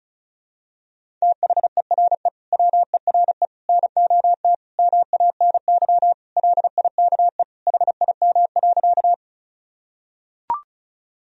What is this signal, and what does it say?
Morse code, heard as a keyed tone, about 34 wpm, 700 Hz. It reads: THERE WERE NOT MANY LIKE HIM.
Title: Sentences from Top 1000 Words - Review Part 1 - 35wpm